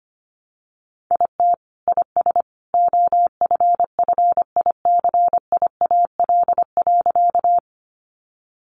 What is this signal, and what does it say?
Morse code, heard as a keyed tone, about 25 wpm, 700 Hz. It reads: IT IS OFFICIAL.
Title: Sentences from Top 1000 Words 25wpm